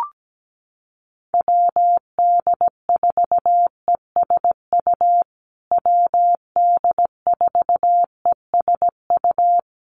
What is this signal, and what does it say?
Morse code, heard as a keyed tone, about 17 wpm, 700 Hz. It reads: WD4ESU WD4ESU
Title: USA Call Signs Repeated 3x 17wpm